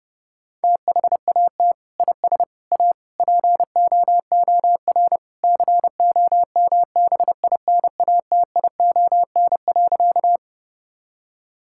Morse code, heard as a keyed tone, about 30 wpm, 700 Hz. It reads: THAT IS A POOR COMBINATION.
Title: Sentences from Top 1000 Words - Review Part 6 - 30wpm